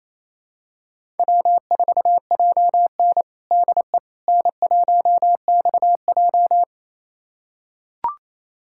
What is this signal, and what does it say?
Morse code, heard as a keyed tone, about 28 wpm, 700 Hz. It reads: W4JN DE N1XJ
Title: USA Callsign DE Callsign - 4 Characters - 28wpm